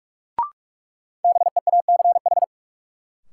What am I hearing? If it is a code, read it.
Morse code, heard as a keyed tone, about 47 wpm, 700 Hz. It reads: BEACH
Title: Sentences from Top 1500 Words - New Word Drill Alt - 45wpm